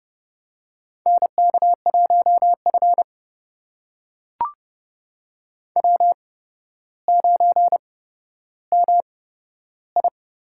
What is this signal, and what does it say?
Morse code, heard as a keyed tone, about 29 wpm, spaced as for 13 wpm, 700 Hz. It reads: NK1F W9MI
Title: USA Call Signs - 4 Characters - 8x Character Spacing - 30wpm